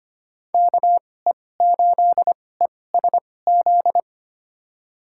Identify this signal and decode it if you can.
Morse code, heard as a keyed tone, about 25 wpm, spaced as for 18 wpm, 700 Hz. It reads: KE8ESZ
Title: USA Call Signs - 6 Characters - 2x Character Spacing - 25wpm